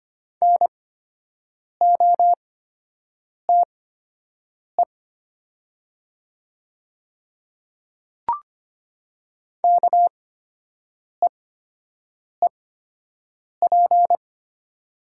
Morse code, heard as a keyed tone, about 25 wpm, spaced as for 7 wpm, 700 Hz. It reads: NOTE KEEP
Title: 8x Character Spacing - Top 500 Words - 25wpm